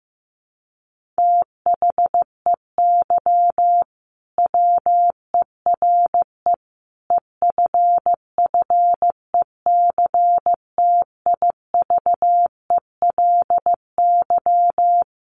Morse code, heard as a keyed tone, about 14 wpm, 700 Hz. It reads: THEY WERE EFFECTIVELY
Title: Sentences from Top 1400 Words 15wpm